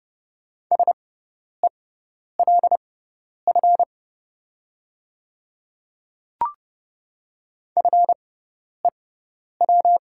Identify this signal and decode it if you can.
Morse code, heard as a keyed tone, about 30 wpm, spaced as for 10 wpm, 700 Hz. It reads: SELF FEW
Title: 6x Character Spacing - Top 500 Words - 30wpm